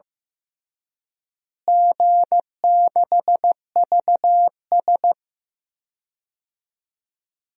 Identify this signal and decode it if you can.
Morse code, heard as a keyed tone, about 15 wpm, 700 Hz. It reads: G6VS
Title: DXCC-100 Call Signs - 4 Characters - Repeated 2x - 15wpm